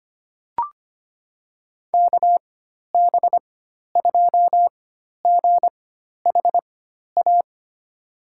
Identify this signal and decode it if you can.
Morse code, heard as a keyed tone, about 25 wpm, spaced as for 12 wpm, 690 Hz. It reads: KB2GHA
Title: USA Call Signs - 4x Character Spacing - 25wpm